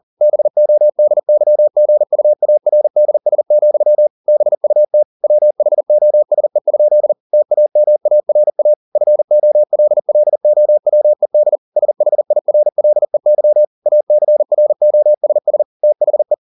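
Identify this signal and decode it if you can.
Morse code, heard as a keyed tone, about 40 wpm, 600 Hz. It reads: BODYGUARDS, BUT WHOSE? TAMARA FOLLOWED SHIRLEY ACROSS THE
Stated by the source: Morse code at 40 wpm, practice text, CW med Gnister.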